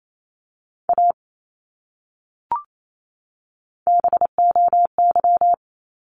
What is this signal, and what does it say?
Morse code, heard as a keyed tone, about 28 wpm, 700 Hz. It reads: A BOY
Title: Top 200 Words 28wpm